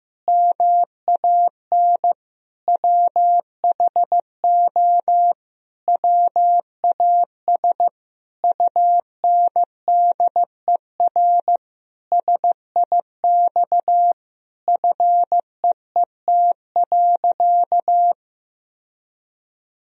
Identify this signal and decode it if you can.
Morse code, heard as a keyed tone, about 15 wpm, 700 Hz. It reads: MAN WHO WAS UNDER SIX FEET.
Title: Sentences from Top 1000 Words - Review Part 2 - 15wpm